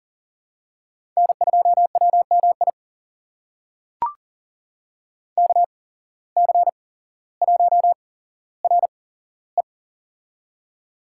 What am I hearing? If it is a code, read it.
Morse code, heard as a keyed tone, about 42 wpm, spaced as for 17 wpm, 700 Hz. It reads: N2WMI KC1RE
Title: USA Call Signs - 5 Characters - 8x Character Spacing - 40wpm